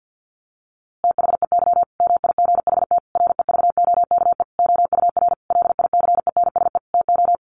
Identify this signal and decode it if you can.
Morse code, heard as a keyed tone, about 50 wpm, 700 Hz. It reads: THEY MIGHT REVOKE OUR LICENSE TO
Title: A Sound Of Thunder 50wpm